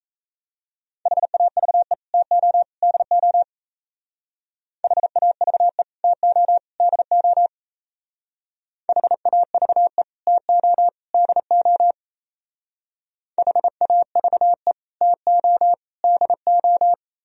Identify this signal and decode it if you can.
Morse code, heard as a keyed tone, about 35 wpm, 700 Hz. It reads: HAVE TO DO HAVE TO DO HAVE TO DO HAVE TO DO